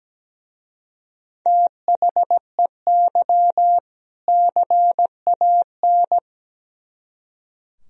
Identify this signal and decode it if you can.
Morse code, heard as a keyed tone, about 20 wpm, 700 Hz. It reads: THEY CAN
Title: Speed-Racing - Sets of 2 Words 17wpm